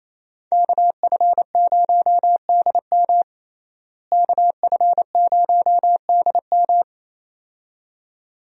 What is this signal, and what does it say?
Morse code, heard as a keyed tone, about 28 wpm, 700 Hz. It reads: KF0DM KF0DM
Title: USA Call Signs - 5 Characters - Repeated 2x - 28wpm